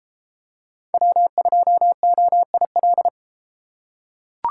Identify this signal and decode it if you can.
Morse code, heard as a keyed tone, about 32 wpm, 700 Hz. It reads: W2OIL